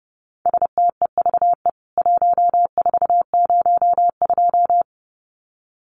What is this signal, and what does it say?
Morse code, heard as a keyed tone, about 30 wpm, 700 Hz. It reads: STEVE 1402
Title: CWT Exchange 30wpm